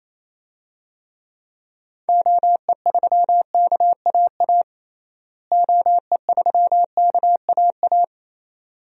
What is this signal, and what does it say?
Morse code, heard as a keyed tone, about 28 wpm, 700 Hz. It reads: OE3KAA OE3KAA
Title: DXCC-100 Call Signs - 6 Characters - Repeated 2x - 28wpm